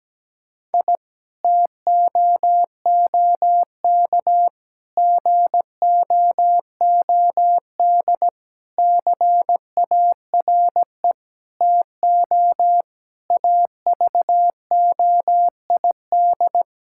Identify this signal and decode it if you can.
Morse code, heard as a keyed tone, about 17 wpm, 700 Hz. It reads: I TOOK GOOD CARE TO AVOID
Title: Sentences from Top 1000 Words - Review Part 6 - 17wpm